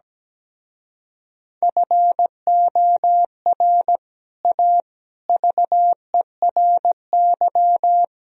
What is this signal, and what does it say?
Morse code, heard as a keyed tone, about 17 wpm, 700 Hz. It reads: FOR A VERY
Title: Speed-Racing - Sets of 5 Words 17wpm